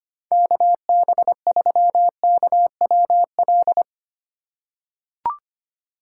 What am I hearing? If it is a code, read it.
Morse code, heard as a keyed tone, about 25 wpm, 700 Hz. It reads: KB3KWL
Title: USA Call Signs - 2x Character Spacing - 25wpm